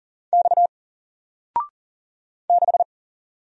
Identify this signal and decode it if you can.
Morse code, heard as a keyed tone, about 39 wpm, 700 Hz. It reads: X 6
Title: Single Letter-Number Alt 40wpm